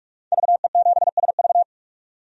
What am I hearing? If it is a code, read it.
Morse code, heard as a keyed tone, about 44 wpm, 690 Hz. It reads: VE7SV
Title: DXCC-100 Call Signs - 5 Characters - 45wpm